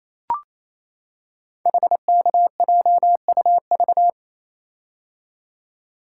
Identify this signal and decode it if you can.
Morse code, heard as a keyed tone, about 28 wpm, 700 Hz. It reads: HKJUV
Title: Code Groups - 5 Letters - 28wpm